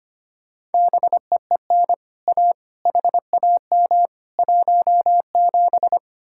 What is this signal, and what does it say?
Morse code, heard as a keyed tone, about 29 wpm, 700 Hz. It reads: BEEN A HAM 17
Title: Speed-Racing Common QSO Phrases Encore - 25wpm